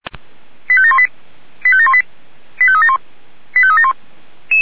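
Five-tone selective call (ZVEI), heard as quick runs of five tones, tones of about 70 ms. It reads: 85718 85718 85371 85371
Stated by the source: ZVEI five-tone selective call received on an SDR, fire brigade alerting.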